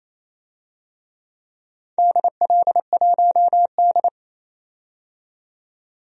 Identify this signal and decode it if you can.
Morse code, heard as a keyed tone, about 28 wpm, 700 Hz. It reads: DL1D
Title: DXCC-100 Call Signs - 4 Characters - 28wpm